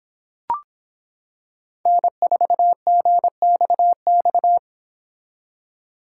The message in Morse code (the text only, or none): N4GXX